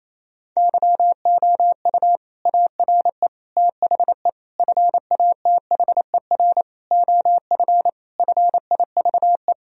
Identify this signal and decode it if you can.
Morse code, heard as a keyed tone, about 28 wpm, 700 Hz. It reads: YOU ARE THE FATHER OF FIVE
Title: Sentences from Top 1000 Words - Review Part 2 - 28wpm